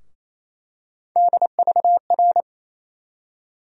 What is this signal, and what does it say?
Morse code, heard as a keyed tone, about 27 wpm, 700 Hz. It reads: DVR
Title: USA Call Sign Postfixes - 3 Letters 28wpm